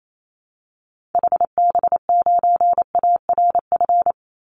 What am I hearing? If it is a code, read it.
Morse code, heard as a keyed tone, about 28 wpm, 700 Hz. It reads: HB9ARF
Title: Call Signs - 28wpm